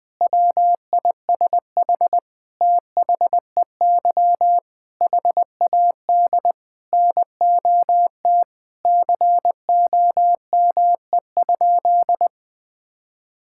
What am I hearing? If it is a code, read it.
Morse code, heard as a keyed tone, about 21 wpm, 700 Hz. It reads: WISH THEY HAD NOT COME?
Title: Sentences from Top 1600 Words 20wpm